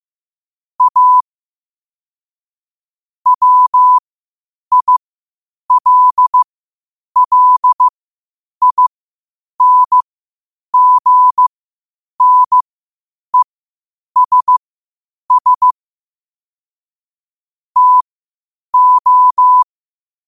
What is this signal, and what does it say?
Morse code, heard as a keyed tone, about 15 wpm, spaced as for 8 wpm, 1000 Hz. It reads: A WILLINGNESS TO
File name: qotdfiles / 8wpm